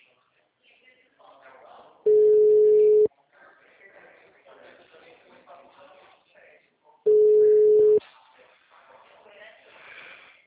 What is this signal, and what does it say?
Ringback tone (Europe, 425 Hz), with speech.